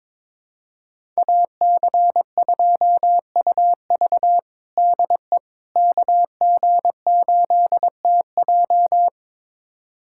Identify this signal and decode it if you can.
Morse code, heard as a keyed tone, about 22 wpm, 700 Hz. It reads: AC2UV DE KG8TJ
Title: USA Callsign DE Callsign - 5 Characters - 22wpm